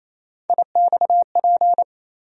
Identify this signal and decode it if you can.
Morse code, heard as a keyed tone, about 28 wpm, 700 Hz. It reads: IXP
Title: USA Call Sign Postfixes - 3 Letters 28wpm